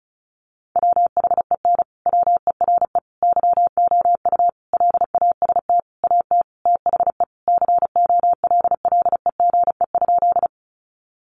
Morse code, heard as a keyed tone, about 34 wpm, 700 Hz. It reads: WHEN WERE YOU LAST AT THE COLLEGE?
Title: Sentences from Top 1300 Words 35wpm